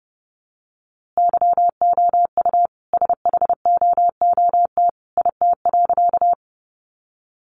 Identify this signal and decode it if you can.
Morse code, heard as a keyed tone, about 30 wpm, 700 Hz. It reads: YOU SHOOT IT.
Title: A Sound Of Thunder 30wpm